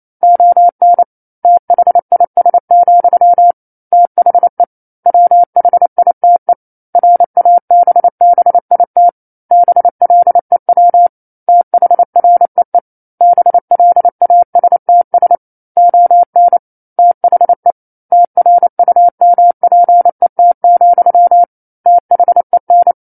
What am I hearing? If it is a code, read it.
Morse code, heard as a keyed tone, about 28 wpm, 700 Hz. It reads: ON THIS, THE WHITE RABBIT BLEW THREE BLASTS ON THE TRUMPET, THEN
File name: 09-Chapter-09 28 WPM